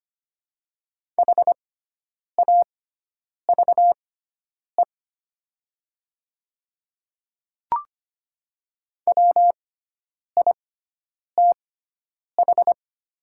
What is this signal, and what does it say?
Morse code, heard as a keyed tone, about 25 wpm, spaced as for 9 wpm, 700 Hz. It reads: HAVE WITH
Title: 6x Character Spacing - Top 100 Words - 25wpm